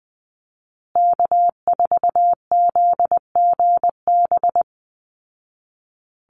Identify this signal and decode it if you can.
Morse code, heard as a keyed tone, about 20 wpm, 700 Hz. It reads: K4ZGB
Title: Call Signs Repeated 2x - 20wpm